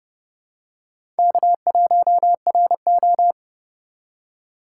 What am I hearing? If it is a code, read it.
Morse code, heard as a keyed tone, about 30 wpm, 700 Hz. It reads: K1RO
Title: K1USN SST QSO Practice - 30wpm